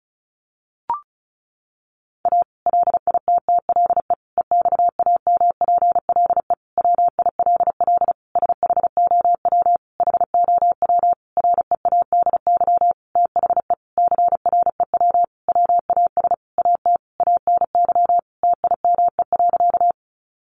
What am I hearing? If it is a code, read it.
Morse code, heard as a keyed tone, about 34 wpm, 700 Hz. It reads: A LITTLE EXAMPLE WILL SHOW HOW READY THE CREW WAS AT ANY TIME.
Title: Sentences from Top 1500 Words 35wpm